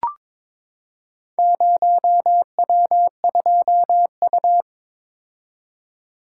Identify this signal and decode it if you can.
Morse code, heard as a keyed tone, about 22 wpm, 700 Hz. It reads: 0W2U
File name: practice / 22wpm / Four Letter-Number Combinations - 22wpm